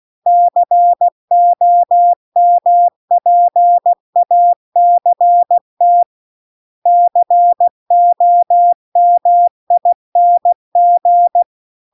Morse code, heard as a keyed tone, about 16 wpm, 700 Hz. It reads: COMPACT COMING